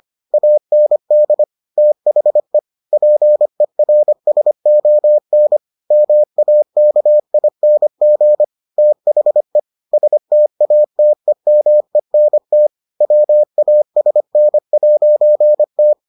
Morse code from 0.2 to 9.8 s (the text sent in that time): AND THE PERSON MAKING THE